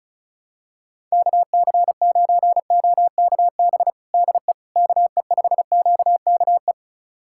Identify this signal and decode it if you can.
Morse code, heard as a keyed tone, about 36 wpm, 700 Hz. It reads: KC9OKB DE KE5QKE
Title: USA Callsign DE Callsign - 6 Characters - 35wpm